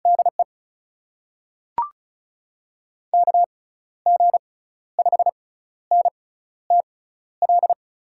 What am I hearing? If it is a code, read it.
Morse code, heard as a keyed tone, about 36 wpm, spaced as for 14 wpm, 700 Hz. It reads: DE KG5NTL
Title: USA Call Signs - 6 Characters - 6x Character Spacing - 35wpm